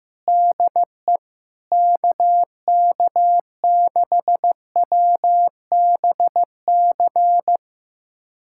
Morse code, heard as a keyed tone, about 15 wpm, 700 Hz. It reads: DE KK6WBC
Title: USA Callsign DE Callsign - 6 Characters - 15wpm